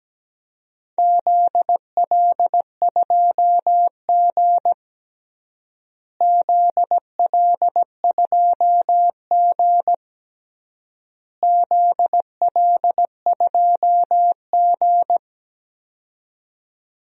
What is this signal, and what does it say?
Morse code, heard as a keyed tone, about 17 wpm, 700 Hz. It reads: ZL2G ZL2G ZL2G
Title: DXCC-100 Call Signs - 4 Characters - Repeated 3x - 17wpm